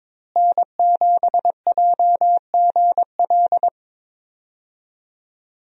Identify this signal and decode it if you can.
Morse code, heard as a keyed tone, about 22 wpm, 700 Hz. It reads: N7JGL
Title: USA Call Signs - 5 Characters - Repeated 3x - 22wpm